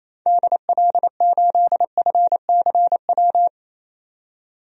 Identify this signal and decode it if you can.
Morse code, heard as a keyed tone, about 28 wpm, 700 Hz. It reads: DL8FCW
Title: DXCC-100 Call Signs - 6 Characters - 28wpm